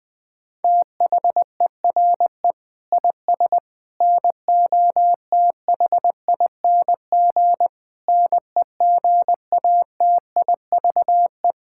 Morse code, heard as a keyed tone, about 20 wpm, 700 Hz. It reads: THERE IS NOTHING NEGATIVE
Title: Sentences from Top 1000 Words - Review Part 6 - 20wpm